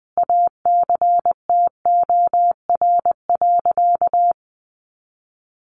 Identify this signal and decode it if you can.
Morse code, heard as a keyed tone, about 20 wpm, 700 Hz. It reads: ACTOR.
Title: Sentences from Top 1500 Words 20wpm